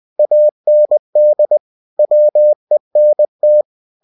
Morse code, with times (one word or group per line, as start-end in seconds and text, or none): 0.2-1.6 AND
2.0-3.6 WENT